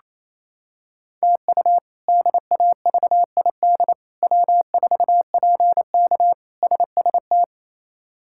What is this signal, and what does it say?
Morse code, heard as a keyed tone, about 28 wpm, 700 Hz. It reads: TU DAVID W4PK SST
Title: K1USN SST QSO Practice - 28wpm